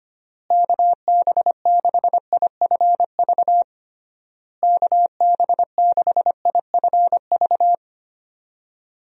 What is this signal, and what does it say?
Morse code, heard as a keyed tone, about 25 wpm, 700 Hz. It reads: KB6IFV KB6IFV
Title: USA Call Signs - 6 Characters - Repeated 3x - 25wpm